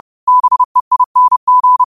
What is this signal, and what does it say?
Morse code, heard as a keyed tone, about 30 wpm, 1000 Hz. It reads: BEING
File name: qotdfiles / 30wpm